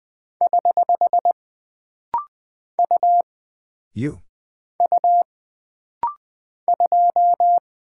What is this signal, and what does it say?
Morse code, heard as a keyed tone, about 20 wpm, 700 Hz. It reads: <HH> U U 2
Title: Single Character (Letter:Number:Punctuation:Prosign) - Rapid-Fire - 20wpm